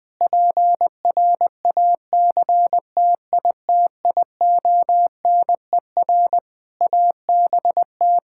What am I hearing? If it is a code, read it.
Morse code, heard as a keyed tone, about 20 wpm, 700 Hz. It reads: PRACTITIONER ABT
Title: Practice QSO - Exchange 3 - More Info - 20wpm